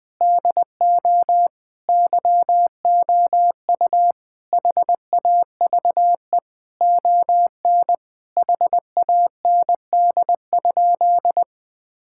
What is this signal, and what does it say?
Morse code, heard as a keyed tone, about 20 wpm, 700 Hz. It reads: DO YOU HAVE ON HAND?
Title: Sentences from Top 1000 Words - Review Part 5 - 20wpm